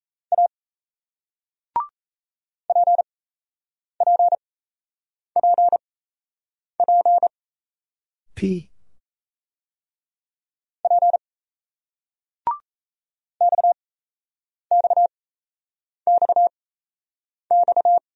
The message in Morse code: A PPPP P XXXX

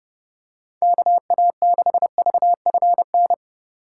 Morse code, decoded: KA6VFN

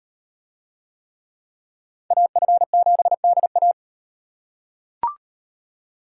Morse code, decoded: AF7DA

